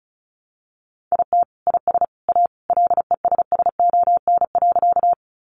Morse code, heard as a keyed tone, about 34 wpm, 700 Hz. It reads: IT IS A LESSON.
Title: Sentences from Top 1200 Words 35wpm